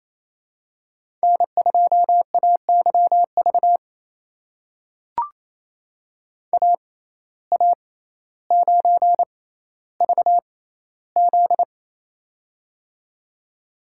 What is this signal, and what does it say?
Morse code, heard as a keyed tone, about 28 wpm, spaced as for 14 wpm, 700 Hz. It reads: N2AYV AA9VZ
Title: USA Call Signs - 5 Characters - 6x Character Spacing - 28wpm